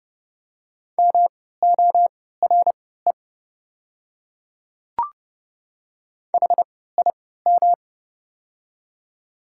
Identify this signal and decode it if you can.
Morse code, heard as a keyed tone, about 30 wpm, spaced as for 17 wpm, 700 Hz. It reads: MORE HIM
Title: 3x Character Spacing - Top 100 Words - 30wpm